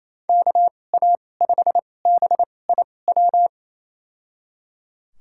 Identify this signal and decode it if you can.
Morse code, heard as a keyed tone, about 28 wpm, spaced as for 20 wpm, 700 Hz. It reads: KA5BIW
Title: USA Call Signs - 6 Characters - 2x Character Spacing - 28wpm